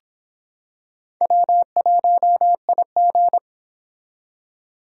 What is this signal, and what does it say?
Morse code, heard as a keyed tone, about 25 wpm, 700 Hz. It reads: W1IG